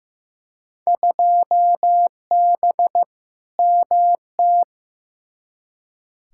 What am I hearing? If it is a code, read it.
Morse code, heard as a keyed tone, about 15 wpm, 700 Hz. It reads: 2B MT
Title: Field Day Exchanges Encore - 15wpm